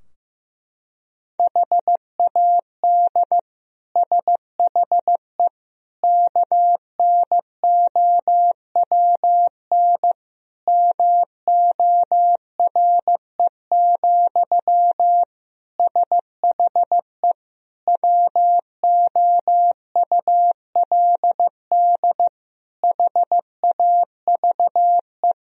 Morse code, heard as a keyed tone, about 15 wpm, 700 Hz. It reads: HAD SHE KNOWN MORE, SHE WOULD HAVE
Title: Sentences from Top 1000 Words - Review Part 3 - 15wpm